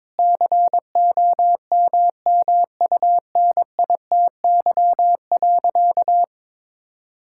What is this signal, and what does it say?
Morse code, heard as a keyed tone, about 21 wpm, 700 Hz. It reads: COMMUNITY.
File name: Difficult Sentences with Hints - 22wpm